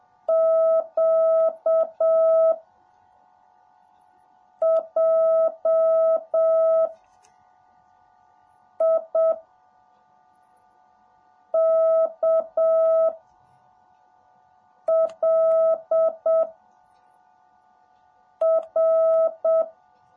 Morse code, heard as a keyed tone, about 7 wpm, 650 Hz. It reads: Q J I K L R
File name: jh8cbh / 2021cw / cw18a